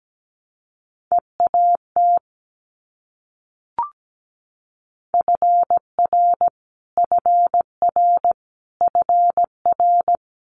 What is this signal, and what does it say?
Morse code, heard as a keyed tone, about 17 wpm, 700 Hz. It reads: EAT FR FR FR
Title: Multi-letter Phonemes Repeated 3x 17wpm